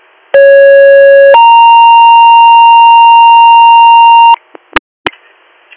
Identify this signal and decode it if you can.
Two-tone page, heard as a one-second tone, then a three-second tone, about 550 then 930 Hz. It is heard from 0.3 to 4.4 s.